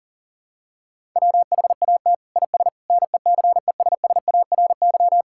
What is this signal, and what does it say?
Morse code, heard as a keyed tone, about 40 wpm, 700 Hz. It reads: WHAT IS NECESSARY
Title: Sentences from Top 1000 Words - Review Part 5 - 40wpm